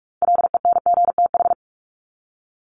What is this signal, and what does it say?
Morse code, heard as a keyed tone, about 44 wpm, 700 Hz. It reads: LENGTH
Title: Top 751-1000 Words 45wpm